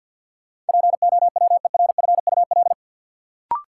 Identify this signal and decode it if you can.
Morse code, heard as a keyed tone, about 51 wpm, 700 Hz. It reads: POWERFUL
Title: Sets of 5 Words Encore - New Word Drill - 50wpm